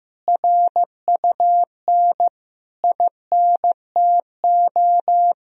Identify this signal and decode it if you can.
Morse code, heard as a keyed tone, about 16 wpm, 700 Hz. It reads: RUN INTO
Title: Sentences from Top 1000 Words - Review Part 1 - 15wpm